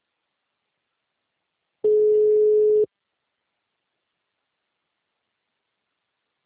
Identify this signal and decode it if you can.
Ringback tone (Europe, 425 Hz).
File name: out- -2007- - - .38